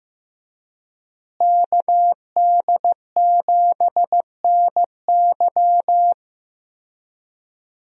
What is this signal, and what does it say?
Morse code, heard as a keyed tone, about 15 wpm, 700 Hz. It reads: KD7NY
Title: USA Call Signs - 5 Characters - Repeated 3x - 15wpm